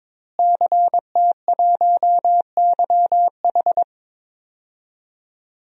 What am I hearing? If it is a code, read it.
Morse code, heard as a keyed tone, about 21 wpm, 700 Hz. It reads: CT1YH
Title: DXCC-100 Call Signs - 5 Characters - Repeated 2x - 22wpm